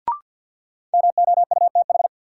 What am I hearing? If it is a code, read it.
Morse code, heard as a keyed tone, about 51 wpm, 700 Hz. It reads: MOUTH
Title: Sentences from Top 1500 Words - New Word Drill - 50wpm